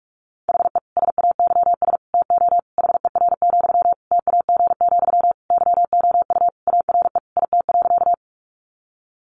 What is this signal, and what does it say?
Morse code, heard as a keyed tone, about 44 wpm, 700 Hz. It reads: HE SAYS TO HER, TAG, YOU ARE IT.
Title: Additional Easy Phrases 45wpm